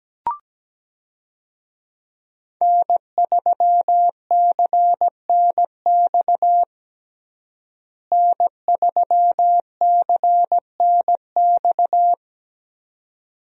Morse code, heard as a keyed tone, about 17 wpm, 700 Hz. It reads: N3CNX N3CNX
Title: USA Call Signs - 5 Characters - Repeated 2x - 17wpm